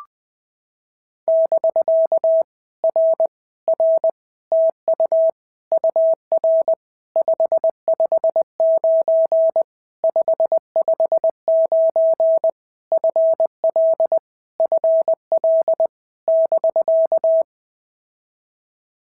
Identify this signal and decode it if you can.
Morse code, heard as a keyed tone, about 19 wpm, 650 Hz. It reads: <BK> R R TU UR 559 559 FL FL <BK>
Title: POTA QSO Practice - 20wpm